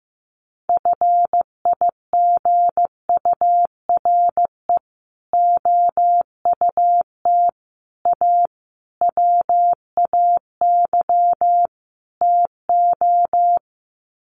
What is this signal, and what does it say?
Morse code, heard as a keyed tone, about 15 wpm, 700 Hz. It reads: FIGURE OUT A WAY TO
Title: Sets of 5 Words Encore2 15wpm